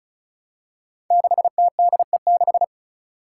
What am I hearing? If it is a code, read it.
Morse code, heard as a keyed tone, about 36 wpm, 700 Hz. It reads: 6TDE6
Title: Five Challenging Letter-Number Combinations 35wpm